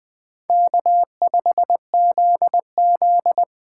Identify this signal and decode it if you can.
Morse code, heard as a keyed tone, about 23 wpm, 700 Hz. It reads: K5ZZ
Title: Speed Racing - USA 1x2 Call Signs 20wpm